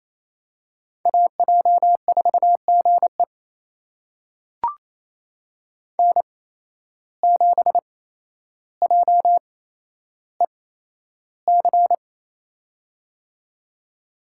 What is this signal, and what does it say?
Morse code, heard as a keyed tone, about 28 wpm, spaced as for 12 wpm, 700 Hz. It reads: AJ4GE N7JEC
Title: USA Call Signs - 5 Characters - 8x Character Spacing - 28wpm